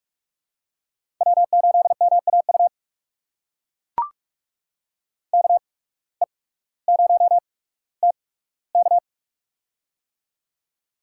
Morse code, heard as a keyed tone, about 44 wpm, spaced as for 19 wpm, 690 Hz. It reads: W8MAU KE0TK
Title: USA Call Signs - 5 Characters - 8x Character Spacing - 45wpm